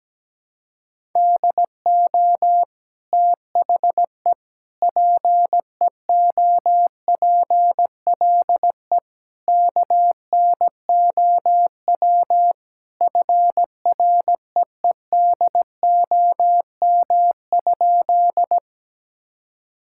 Morse code, heard as a keyed tone, about 17 wpm, 700 Hz. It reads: DO THE PEOPLE KNOW FREEDOM?
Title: Sentences from Top 1000 Words - Review Part 5 - 17wpm